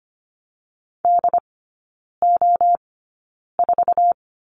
Morse code, heard as a keyed tone, about 25 wpm, spaced as for 9 wpm, 690 Hz. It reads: DO4
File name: Lesson 14e - Number 2 - Three Characters 25wpm